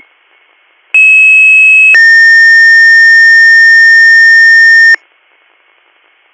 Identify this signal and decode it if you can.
Two-tone page, heard as a one-second tone, then a three-second tone, about 2600 then 1800 Hz. It is heard from 0.9 to 5.0 s.